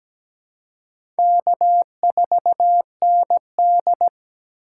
Morse code, heard as a keyed tone, about 17 wpm, 700 Hz. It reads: K4ND
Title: USA Call Signs - 4 Characters - 3x Character Spacing - 17wpm